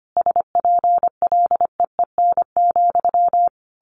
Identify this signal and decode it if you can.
Morse code, heard as a keyed tone, about 25 wpm, 700 Hz. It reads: SPLEEN,
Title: A Sound Of Thunder 25wpm